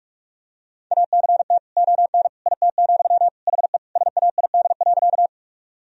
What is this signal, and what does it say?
Morse code, heard as a keyed tone, about 45 wpm, 700 Hz. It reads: ACT ON IT, HE SAID.
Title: Sentences from Top 1000 Words - Review Part 1 - 45wpm